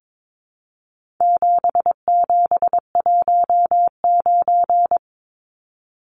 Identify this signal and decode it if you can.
Morse code, heard as a keyed tone, about 22 wpm, 700 Hz. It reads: 7719
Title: Numbers - 4-Digit 22wpm